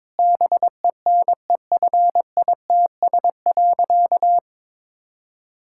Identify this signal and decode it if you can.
Morse code, heard as a keyed tone, about 22 wpm, 700 Hz. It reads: BENEFITS.
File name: Difficult Sentences with Hints - 22wpm